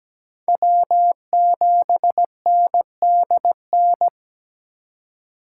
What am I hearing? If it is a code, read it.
Morse code, heard as a keyed tone, about 20 wpm, 700 Hz. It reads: W7NDN
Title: Speed Racing - USA Call Signs 17wpm